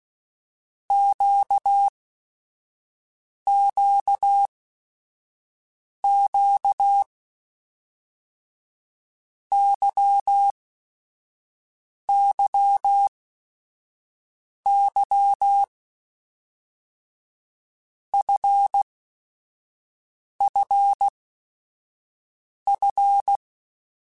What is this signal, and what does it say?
Morse code, heard as a keyed tone, about 16 wpm, spaced as for 5 wpm, 800 Hz. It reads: QQQ YYY FFF
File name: Basic Sets / Basic Letter Groups Collective7 / LFCYQ@5